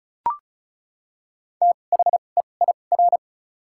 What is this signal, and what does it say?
Morse code, heard as a keyed tone, about 34 wpm, spaced as for 25 wpm, 700 Hz. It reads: THEIR
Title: 2x Character Spacing - Top 200 Words - 35wpm